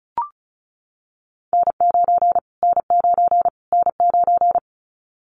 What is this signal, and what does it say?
Morse code, heard as a keyed tone, about 34 wpm, 700 Hz. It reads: N9 N9 N9
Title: Callsign Prefixes for Top 25 DXCC - Repeated 3x - 35wpm